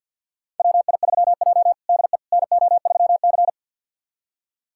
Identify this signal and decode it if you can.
Morse code, heard as a keyed tone, about 46 wpm, 700 Hz. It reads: WI3J DE NO3C
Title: USA Callsign DE Callsign - 4 Characters - 50wpm